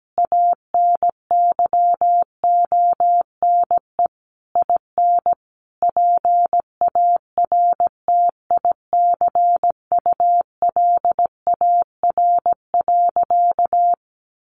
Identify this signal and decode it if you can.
Morse code, heard as a keyed tone, about 17 wpm, 700 Hz. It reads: ANYONE IN PARTICULAR.
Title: Sentences from Top 900 Words 17wpm